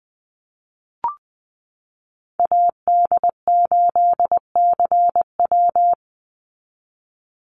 Morse code, heard as a keyed tone, about 20 wpm, 700 Hz. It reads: AD8CW
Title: Call Signs Repeated 3x - 20wpm